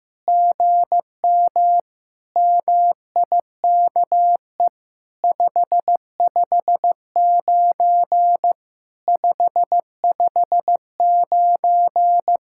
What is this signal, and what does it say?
Morse code, heard as a keyed tone, about 14 wpm, 700 Hz. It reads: GM MIKE 559 559
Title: SOTA QSO Practice 15wpm